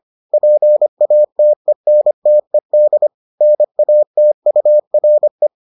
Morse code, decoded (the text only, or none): PATENTED NATURE